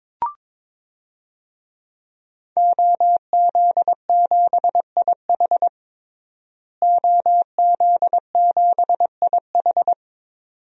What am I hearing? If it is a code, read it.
Morse code, heard as a keyed tone, about 22 wpm, 700 Hz. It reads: OZ7IH OZ7IH
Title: DXCC-100 Call Signs - 5 Characters - Repeated 2x - 22wpm